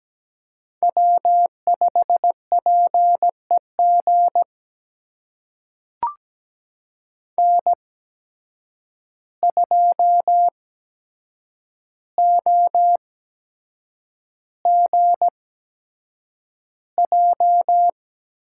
Morse code, heard as a keyed tone, about 17 wpm, spaced as for 7 wpm, 700 Hz. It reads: W5PEG N2OGJ
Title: USA Call Signs - 5 Characters - 8x Character Spacing - 17wpm